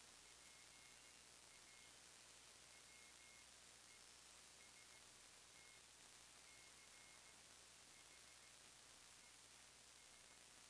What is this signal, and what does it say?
Morse code, heard as a keyed tone, about 15 wpm, spaced as for 10 wpm, 2150 Hz, with noise. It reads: PA WEST CHES
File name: sdr / WINRAD WB5RVZ PSK Sample 40m